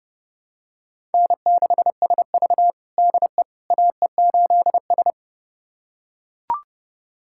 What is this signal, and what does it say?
Morse code, heard as a keyed tone, about 30 wpm, 700 Hz. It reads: N6SV DE AE8S
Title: USA Callsign DE Callsign - 4 Characters - 30wpm